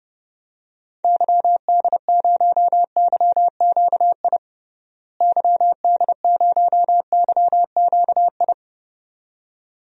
Morse code, heard as a keyed tone, about 31 wpm, 700 Hz. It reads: YD0YQI YD0YQI
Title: DXCC-100 Call Signs - 6 Characters - Repeated 2x - 30wpm